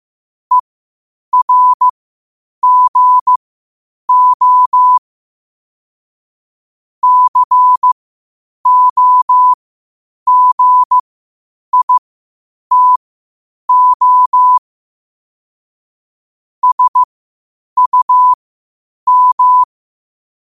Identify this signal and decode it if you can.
Morse code, heard as a keyed tone, about 15 wpm, spaced as for 8 wpm, 1000 Hz. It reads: ERGO COGITO SUM
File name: qotdfiles / 8wpm